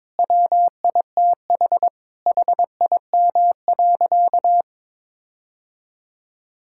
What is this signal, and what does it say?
Morse code, heard as a keyed tone, about 21 wpm, 700 Hz. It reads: WITH HIM.
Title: Sentences from Top 1600 Words 22wpm